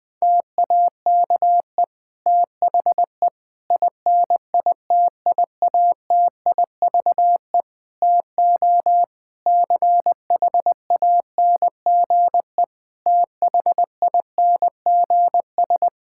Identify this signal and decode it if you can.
Morse code, heard as a keyed tone, about 20 wpm, 700 Hz. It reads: TAKE THE INITIATIVE TO CHANGE THINGS
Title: Difficult Sentences with Hints - 20wpm